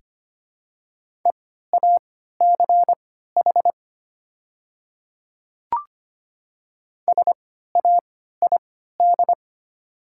Morse code, heard as a keyed tone, about 25 wpm, spaced as for 14 wpm, 700 Hz. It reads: EACH SAID
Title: 3x Character Spacing - Top 200 Words - 25wpm